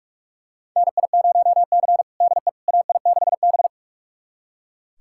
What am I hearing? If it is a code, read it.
Morse code, heard as a keyed tone, about 42 wpm, 700 Hz. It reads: NI0C DE AI6B
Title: USA Callsign DE Callsign - 4 Characters - 45wpm